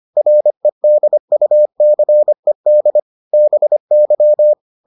Morse code, heard as a keyed tone, about 25 wpm, 600 Hz. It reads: REDUCED BY